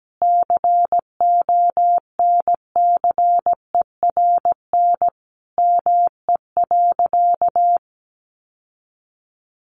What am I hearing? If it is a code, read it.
Morse code, heard as a keyed tone, about 17 wpm, 700 Hz. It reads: CONCERN ME.
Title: Sentences from Top 1000 Words 17wpm